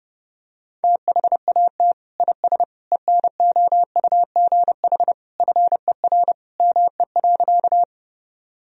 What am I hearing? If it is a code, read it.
Morse code, heard as a keyed tone, about 32 wpm, 700 Hz. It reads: THAT IS ENOUGH FER ME.
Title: Sentences from Top 400 Words and Abbreviations 30wpm